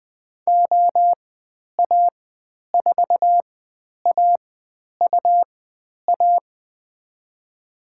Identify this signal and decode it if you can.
Morse code, heard as a keyed tone, about 21 wpm, spaced as for 10 wpm, 700 Hz. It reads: OA4AUA
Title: 109 Lesson 18g - Letter m - Call Signs 20wpm